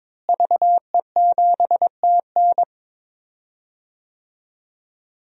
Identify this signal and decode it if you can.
Morse code, heard as a keyed tone, about 22 wpm, 700 Hz. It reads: VE7TN
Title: DXCC-100 Call Signs - 5 Characters - Repeated 3x - 22wpm